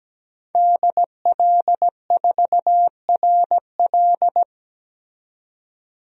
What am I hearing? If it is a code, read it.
Morse code, heard as a keyed tone, about 17 wpm, 700 Hz. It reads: DL4RL
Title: DXCC-100 Call Signs - 5 Characters - 17wpm